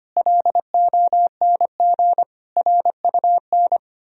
Morse code, heard as a keyed tone, about 31 wpm, 700 Hz. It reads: LONG RUN
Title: Speed-Racing - Sets of 4 Words 25wpm